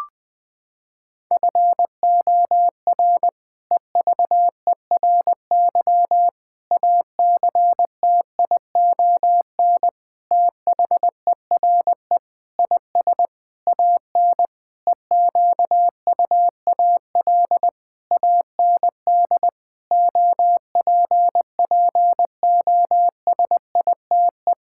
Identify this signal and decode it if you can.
Morse code, heard as a keyed tone, about 20 wpm, 700 Hz. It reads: FOR EVERY ACTION THERE IS AN EQUAL AND OPPOSITE
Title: Sentences with 8-Word Phrases - 20wpm